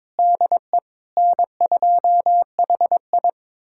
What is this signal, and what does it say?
Morse code, heard as a keyed tone, about 22 wpm, 700 Hz. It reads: DE N2HI
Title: USA Callsign DE Callsign - 4 Characters - 22wpm